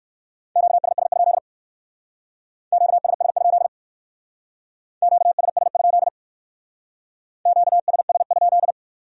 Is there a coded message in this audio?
Yes: Morse code from 0.6 to 8.7 s, heard as a keyed tone, about 57 wpm, 700 Hz. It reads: QSS? QSS? QSS? QSS?